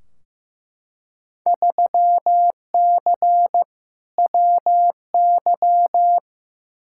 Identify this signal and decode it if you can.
Morse code, heard as a keyed tone, about 15 wpm, 700 Hz. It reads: 3C WY
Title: Field Day Exchanges Encore - 15wpm